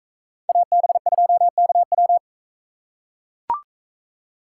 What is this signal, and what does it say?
Morse code, heard as a keyed tone, about 40 wpm, 700 Hz. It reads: AB2KW